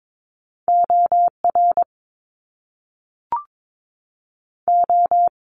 Morse code, heard as a keyed tone, about 22 wpm, 700 Hz. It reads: OR O
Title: Top Two Letter Combinations in Words 22wpm